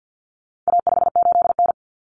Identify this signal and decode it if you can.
Morse code, heard as a keyed tone, about 50 wpm, 700 Hz. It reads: A58N
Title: Common Cut Numbers - 4-Digit 50wpm